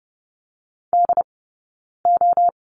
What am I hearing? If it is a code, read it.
Morse code, heard as a keyed tone, about 30 wpm, 700 Hz. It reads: D O
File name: Lesson 19d - Letter w - Two Characters 30wpm